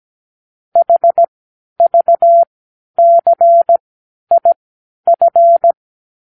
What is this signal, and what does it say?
Morse code, heard as a keyed tone, about 17 wpm, spaced as for 10 wpm, 680 Hz. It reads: HVCIF
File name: k7qo / Lesson 36 (A-X)